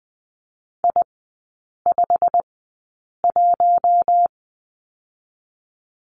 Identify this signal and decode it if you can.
Morse code, heard as a keyed tone, about 20 wpm, spaced as for 8 wpm, 700 Hz. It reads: I51
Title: 087 Lesson 15e - Number 5 - Three Characters 20wpm